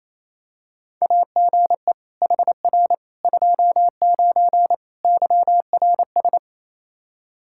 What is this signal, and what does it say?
Morse code, heard as a keyed tone, about 28 wpm, 700 Hz. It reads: AGE HR 29 YRS
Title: Practice QSO - Exchange 3 - More Info - 28wpm